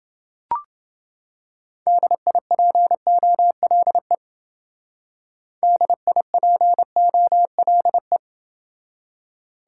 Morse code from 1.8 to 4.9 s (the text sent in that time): DIPOLE